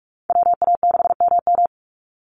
Morse code, heard as a keyed tone, about 44 wpm, 700 Hz. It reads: WA6MM
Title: NA SOTA Activator Callsigns 45wpm